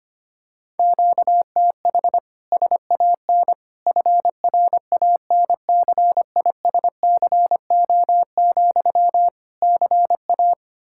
Morse code, heard as a keyed tone, about 25 wpm, 700 Hz. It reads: QTH SAN FRANCISCO, CA
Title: Practice QSO - Exchange 1 - Beginning - 25wpm